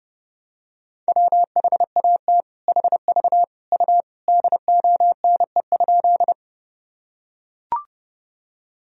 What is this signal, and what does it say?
Morse code, heard as a keyed tone, about 32 wpm, 700 Hz. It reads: WHAT HV U DONE?
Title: Sentences from Top 500 Words and Abbreviations 30wpm